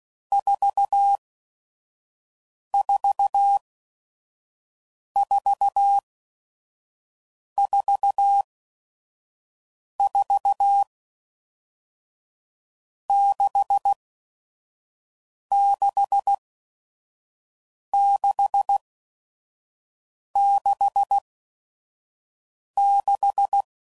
Morse code, heard as a keyed tone, about 16 wpm, spaced as for 5 wpm, 800 Hz. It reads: 44444 66666
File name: Groups / Basic Letter Groups / VB46